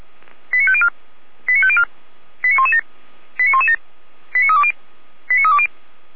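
Five-tone selective call (ZVEI), heard as quick runs of five tones, tones of about 70 ms. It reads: 89504 89504 89117 89117 89320 89320
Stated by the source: SDR-received ZVEI five-tone selective call, code 89071.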